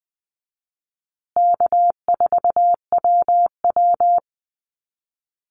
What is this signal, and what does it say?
Morse code, heard as a keyed tone, about 20 wpm, 700 Hz. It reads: K4WW
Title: Easy Call Signs Repeated 3x - 20wpm